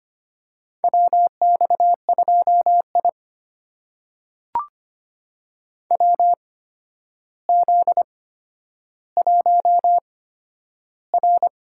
Morse code, heard as a keyed tone, about 25 wpm, spaced as for 11 wpm, 700 Hz. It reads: WX2I WZ1R
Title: USA Call Signs - 4 Characters - 8x Character Spacing - 25wpm